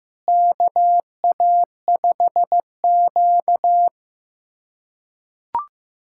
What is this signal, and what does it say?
Morse code, heard as a keyed tone, about 15 wpm, 700 Hz. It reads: KA5Q
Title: USA Callsign DE Callsign - 4 Characters - 15wpm